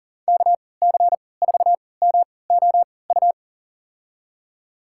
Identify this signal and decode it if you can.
Morse code, heard as a keyed tone, about 40 wpm, spaced as for 23 wpm, 700 Hz. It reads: KC4MOU
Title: USA Call Signs - 6 Characters - 3x Character Spacing - 40wpm